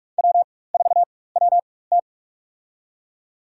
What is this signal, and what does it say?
Morse code, heard as a keyed tone, about 46 wpm, spaced as for 21 wpm, 700 Hz. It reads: W4WT
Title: USA Call Signs - 4x Character Spacing - 45wpm